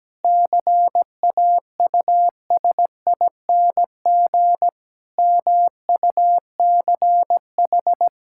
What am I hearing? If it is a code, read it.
Morse code, heard as a keyed tone, about 17 wpm, 700 Hz. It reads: CAUSING MUCH
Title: Pangram Challenge v2 - 17wpm